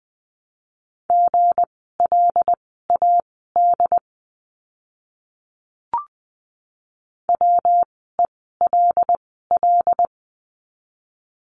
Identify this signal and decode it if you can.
Morse code, heard as a keyed tone, about 20 wpm, spaced as for 15 wpm, 700 Hz. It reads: GLAD WELL
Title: 160 Lesson 26f - Letter g - Words 20wpm